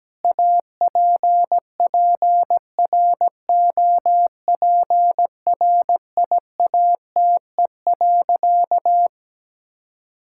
Morse code, heard as a keyed tone, about 17 wpm, 700 Hz. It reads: APPROPRIATE.
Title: Sentences from Top 1000 Words - Review Part 5 - 17wpm